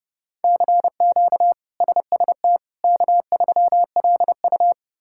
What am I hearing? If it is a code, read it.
Morse code, heard as a keyed tone, about 30 wpm, 700 Hz. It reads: CQ SST K3LU
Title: K1USN SST QSO Practice - 30wpm